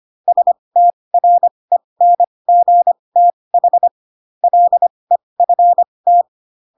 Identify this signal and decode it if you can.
Morse code, heard as a keyed tone, about 25 wpm, spaced as for 20 wpm, 700 Hz. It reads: STRENGTH LEFT